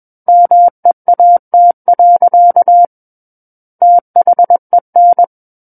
Morse code, heard as a keyed tone, about 21 wpm, 700 Hz. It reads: MEAT. THEN
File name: Book One / 12-Chapter 12 - 21 WPM